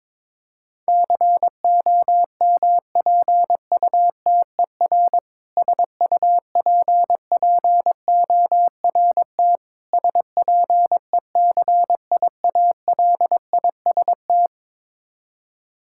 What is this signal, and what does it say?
Morse code, heard as a keyed tone, about 21 wpm, 700 Hz. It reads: COMPUTER SUPPORT SPECIALIST
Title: K1USN SST QSO Practice 22wpm